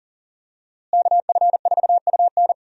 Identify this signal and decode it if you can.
Morse code, heard as a keyed tone, about 40 wpm, 700 Hz. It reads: KF4UN
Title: USA Call Signs - 5 Characters - Repeated 3x - 40wpm